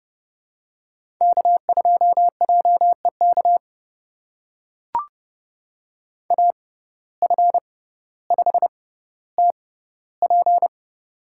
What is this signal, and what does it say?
Morse code, heard as a keyed tone, about 30 wpm, spaced as for 15 wpm, 700 Hz. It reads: K2JEK AF5TP
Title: USA Call Signs - 5 Characters - 6x Character Spacing - 30wpm